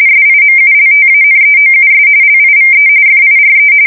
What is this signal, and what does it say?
RTTY, about 45.45 baud. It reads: SIMULTANEOUS LIVE FIRE
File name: fsk170 / USS-New-Jersey